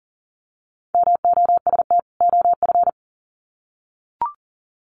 Morse code, heard as a keyed tone, about 40 wpm, 700 Hz. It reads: MOST OF
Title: Sets of 2 Words 40wpm